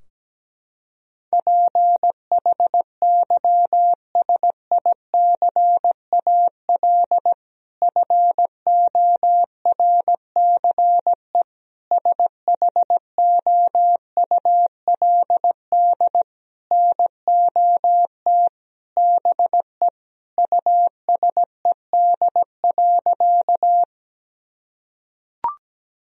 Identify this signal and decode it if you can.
Morse code, heard as a keyed tone, about 17 wpm, 700 Hz. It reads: PHYSICAL FORCE SHOULD NOT BE USED.
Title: Sentences from Top 1000 Words - Review Part 4 - 17wpm